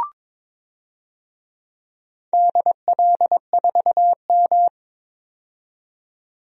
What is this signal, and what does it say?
Morse code, heard as a keyed tone, about 23 wpm, 690 Hz. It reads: DL4M
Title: DXCC-100 Call Signs - 4 Characters - 22wpm